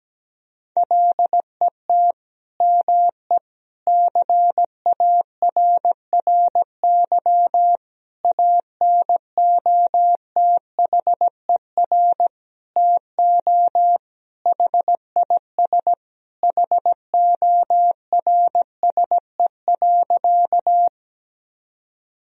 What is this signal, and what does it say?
Morse code, heard as a keyed tone, about 17 wpm, 700 Hz. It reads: LET ME CARRY ANOTHER TO HIS HORSE.
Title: Sentences from Top 1000 Words - Review Part 1 - 17wpm